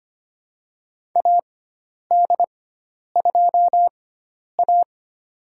Morse code, heard as a keyed tone, about 25 wpm, spaced as for 10 wpm, 700 Hz. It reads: AD2A
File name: Lesson 16g - Letter c - Call Signs 25wpm